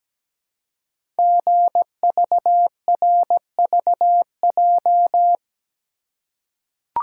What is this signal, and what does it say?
Morse code, heard as a keyed tone, about 17 wpm, 700 Hz. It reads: GVRVJ
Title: Code Groups - 5 Letter:Numbers - 17wpm